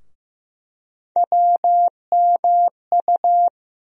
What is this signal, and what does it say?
Morse code, heard as a keyed tone, about 16 wpm, 700 Hz. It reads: WMU
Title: USA Call Sign Postfixes - 3 Letters 15wpm